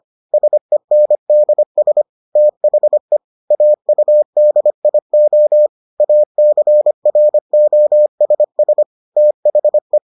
Morse code, heard as a keyed tone, about 25 wpm, 600 Hz. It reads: SENDS THE AUDIO ACROSS THE